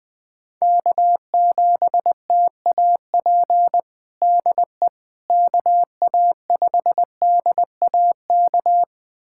Morse code, heard as a keyed tone, about 20 wpm, 700 Hz. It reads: K7TAP DE KA5DAK
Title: USA Callsign DE Callsign 20wpm